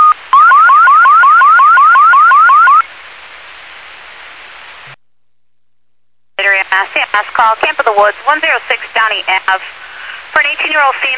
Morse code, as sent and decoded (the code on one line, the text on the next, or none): none